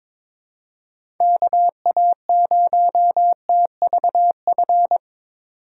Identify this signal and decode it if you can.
Morse code, heard as a keyed tone, about 21 wpm, 700 Hz. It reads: KA0TVF
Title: USA Call Signs - 6 Characters - 2x Character Spacing - 22wpm